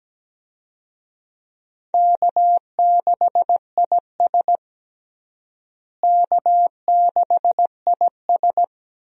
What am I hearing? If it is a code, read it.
Morse code, heard as a keyed tone, about 17 wpm, 700 Hz. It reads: K6IS K6IS
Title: USA Call Signs - 4 Characters - Repeated 3x - 17wpm